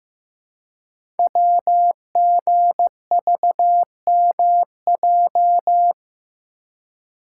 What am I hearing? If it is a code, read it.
Morse code, heard as a keyed tone, about 15 wpm, 700 Hz. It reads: WGVMJ
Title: Code Groups - 5 Letters - 15wpm